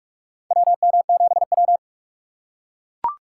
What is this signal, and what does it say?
Morse code, heard as a keyed tone, about 44 wpm, 700 Hz. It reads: WM7W
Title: USA Call Signs - 6x Character Spacing - 45wpm